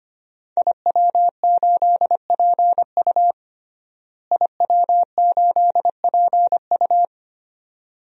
Morse code, heard as a keyed tone, about 25 wpm, 700 Hz. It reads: IW8PU IW8PU
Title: DXCC-100 Call Signs - 5 Characters - Repeated 2x - 25wpm